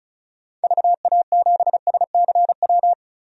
Morse code, heard as a keyed tone, about 36 wpm, 700 Hz. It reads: VA7SCW
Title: DXCC-100 Call Signs - 6 Characters - 35wpm